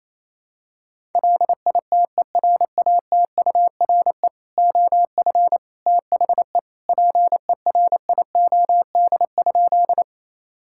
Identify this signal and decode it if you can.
Morse code, heard as a keyed tone, about 28 wpm, 700 Hz. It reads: LITERATURE OF THE PERIOD?
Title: Sentences from Top 1000 Words - Review Part 5 - 28wpm